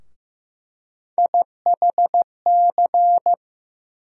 Morse code, heard as a keyed tone, about 15 wpm, 700 Hz. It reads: IHC
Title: USA Call Sign Postfixes - 3 Letters 15wpm